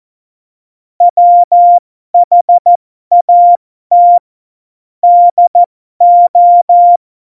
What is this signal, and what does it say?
Morse code, heard as a keyed tone, about 14 wpm, spaced as for 12 wpm, 700 Hz. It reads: WHAT DO
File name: Morsekurs / Witze